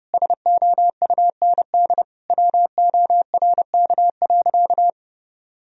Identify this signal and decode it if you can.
Morse code, heard as a keyed tone, about 32 wpm, 700 Hz. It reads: SOUND WORK.
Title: Sentences from Top 1000 Words - Review Part 2 - 30wpm